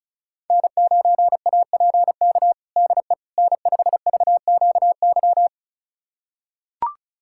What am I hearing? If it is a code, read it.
Morse code, heard as a keyed tone, about 36 wpm, 700 Hz. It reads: N9APK DE N5VQY
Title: USA Callsign DE Callsign - 5 Characters - 35wpm